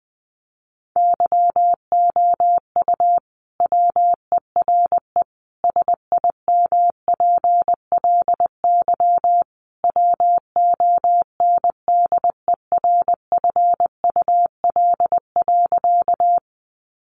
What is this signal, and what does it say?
Morse code, heard as a keyed tone, about 20 wpm, 700 Hz. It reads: YOU WERE SIMPLY WONDERFUL.
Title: Sentences from Top 1100 Words 20wpm